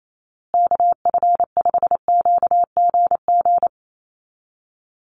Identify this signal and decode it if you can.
Morse code, heard as a keyed tone, about 28 wpm, 700 Hz. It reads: KF5QGG
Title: USA Call Signs Repeated 2x 28wpm